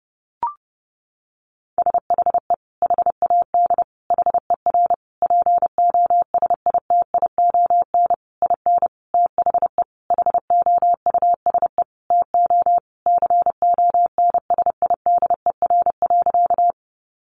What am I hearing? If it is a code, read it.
Morse code, heard as a keyed tone, about 28 wpm, 700 Hz. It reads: SHE HAD HER POSITION IN THE HOUSE TO CONSIDER.
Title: Sentences from Top 900 Words 30wpm